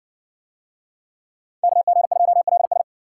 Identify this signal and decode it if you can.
Morse code, heard as a keyed tone, about 57 wpm, 700 Hz. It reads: KG2LS